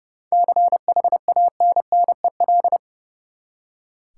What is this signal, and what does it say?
Morse code, heard as a keyed tone, about 30 wpm, 700 Hz. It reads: CHANNEL
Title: Seven Letter Words - 30wpm